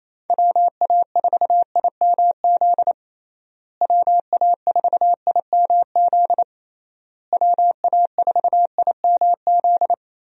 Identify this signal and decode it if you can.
Morse code, heard as a keyed tone, about 28 wpm, 700 Hz. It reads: WA4IMZ WA4IMZ WA4IMZ
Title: USA Call Signs - 6 Characters - Repeated 3x - 28wpm